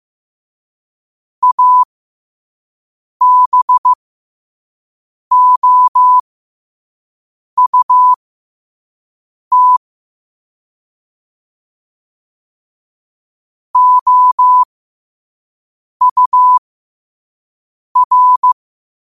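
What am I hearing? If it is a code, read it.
Morse code, heard as a keyed tone, about 15 wpm, spaced as for 5 wpm, 1000 Hz. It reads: ABOUT OUR
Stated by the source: Morse sent at 5 wpm.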